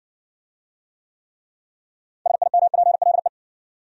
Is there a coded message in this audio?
Yes: Morse code from 2.3 to 3.3 s, heard as a keyed tone, about 57 wpm, 700 Hz. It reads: SIMPLE